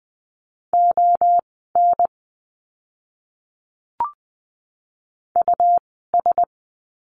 Morse code, heard as a keyed tone, about 20 wpm, spaced as for 14 wpm, 700 Hz. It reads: ON US
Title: 146 Lesson 24f - Letter y - Words 20wpm